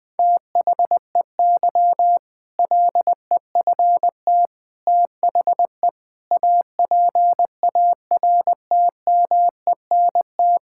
Morse code, heard as a keyed tone, about 20 wpm, 700 Hz. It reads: THEY LEFT THE APARTMENT
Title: Sentences from Top 1000 Words - Review Part 6 - 20wpm